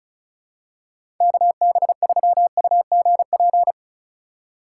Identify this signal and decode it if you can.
Morse code, heard as a keyed tone, about 34 wpm, 700 Hz. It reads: KB3UGP
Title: USA Call Signs - 6 Characters - 4x Character Spacing - 35wpm